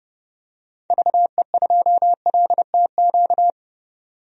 Morse code, heard as a keyed tone, about 30 wpm, 700 Hz. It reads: VE2LTQ
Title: DXCC-100 Call Signs - 6 Characters - 30wpm